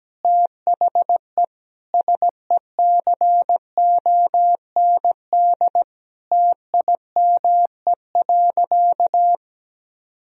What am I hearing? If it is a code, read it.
Morse code, heard as a keyed tone, about 17 wpm, 700 Hz. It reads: THE SECOND TIME.
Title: Sentences from Top 1000 Words - Review Part 1 - 17wpm